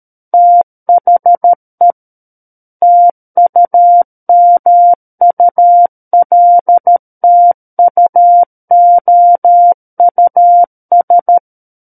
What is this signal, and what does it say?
Morse code, heard as a keyed tone, about 13 wpm, 700 Hz. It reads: THE TUMULTUOUS
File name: Book One / 04-Chapter 4 - 13 WPM